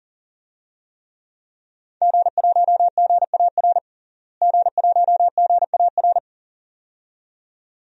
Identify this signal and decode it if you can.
Morse code, heard as a keyed tone, about 40 wpm, 700 Hz. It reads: G1GAR G1GAR
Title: DXCC-100 Call Signs - 5 Characters - Repeated 2x - 40wpm